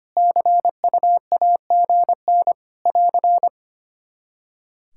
Morse code, heard as a keyed tone, about 25 wpm, 700 Hz. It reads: CUAGN <AR>
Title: K1USN SST QSO Practice 25wpm